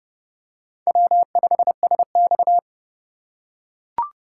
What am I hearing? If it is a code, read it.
Morse code, heard as a keyed tone, about 29 wpm, 700 Hz. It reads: W5SX